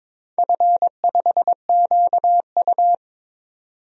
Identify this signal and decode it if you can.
Morse code, heard as a keyed tone, about 22 wpm, 690 Hz. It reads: F5QU
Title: DXCC-100 Call Signs - 4 Characters - 22wpm